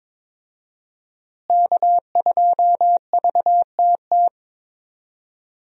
Morse code, heard as a keyed tone, about 22 wpm, 700 Hz. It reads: K2VTT
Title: USA Call Signs - 5 Characters - Repeated 2x - 22wpm